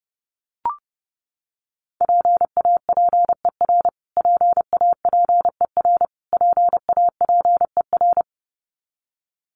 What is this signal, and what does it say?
Morse code, heard as a keyed tone, about 30 wpm, 700 Hz. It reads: PAPER PAPER PAPER
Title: Top 500 Words Repeated 30wpm